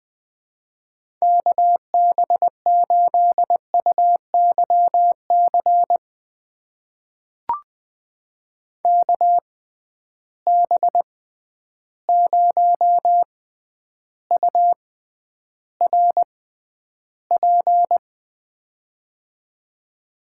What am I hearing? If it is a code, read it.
Morse code, heard as a keyed tone, about 20 wpm, spaced as for 10 wpm, 700 Hz. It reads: KB8UYC KB0URP
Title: USA Call Signs - 6 Characters - 6x Character Spacing - 20wpm